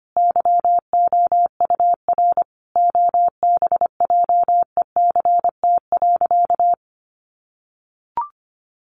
Morse code, heard as a keyed tone, about 25 wpm, 700 Hz. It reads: YOUR OBJECT.
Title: Sentences from Top 1300 Words 25wpm